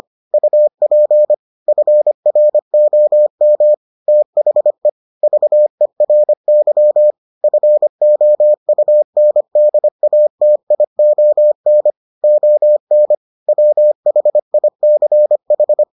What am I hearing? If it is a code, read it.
Morse code, heard as a keyed tone, about 25 wpm, 600 Hz. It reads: UP FROM THE VERY FOUNDATION ON WHICH